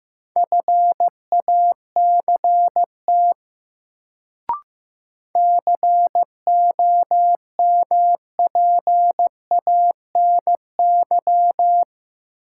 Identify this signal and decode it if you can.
Morse code, heard as a keyed tone, about 15 wpm, 700 Hz. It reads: FACT COMPANY
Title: Sets of 2 Words - New Word Drill - 15wpm